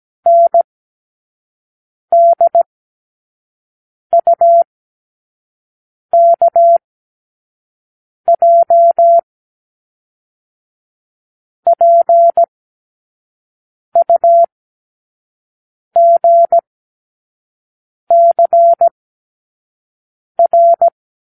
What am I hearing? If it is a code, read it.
Morse code, heard as a keyed tone, about 17 wpm, spaced as for 5 wpm, 680 Hz. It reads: NDUKJ PUGCR
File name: k7qo / Lesson 36 (A-X)